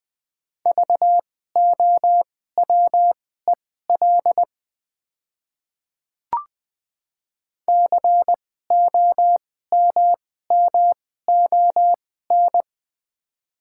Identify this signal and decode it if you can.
Morse code, heard as a keyed tone, about 20 wpm, spaced as for 15 wpm, 700 Hz. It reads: VOWEL COMMON
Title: 2x Character Spacing - Top 500 Words - 20wpm